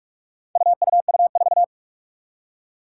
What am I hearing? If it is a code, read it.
Morse code, heard as a keyed tone, about 46 wpm, 700 Hz. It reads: UUU4
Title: Four Challenging Letter-Number Combinations 45wpm